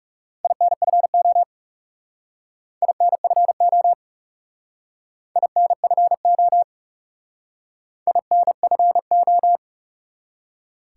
Morse code, heard as a keyed tone, about 36 wpm, 700 Hz. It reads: INFO INFO INFO INFO
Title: Speed-Racing Top 180 QSO Elements 30wpm